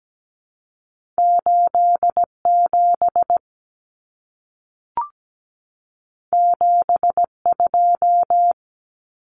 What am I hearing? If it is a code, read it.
Morse code, heard as a keyed tone, about 17 wpm, 700 Hz. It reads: 87 72
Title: Numbers - 2-Digit 17wpm